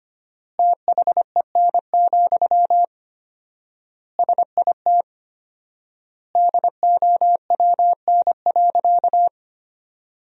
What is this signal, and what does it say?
Morse code, heard as a keyed tone, about 25 wpm, 700 Hz. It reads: THEN, SIT DOWN.
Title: 4x Word-Spacing - Sentences from Top 500 Words - 25wpm